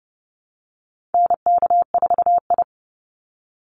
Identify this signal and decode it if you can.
Morse code, heard as a keyed tone, about 30 wpm, 700 Hz. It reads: NK4I
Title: Call Signs Repeated 3x - 30wpm